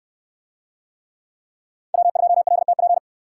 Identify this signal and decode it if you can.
Morse code, heard as a keyed tone, about 69 wpm, 700 Hz. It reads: W2LTP